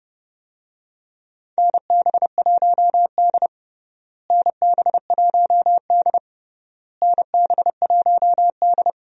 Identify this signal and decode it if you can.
Morse code, heard as a keyed tone, about 32 wpm, 700 Hz. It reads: NB1D NB1D NB1D
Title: USA Call Signs - 4 Characters - Repeated 3x - 30wpm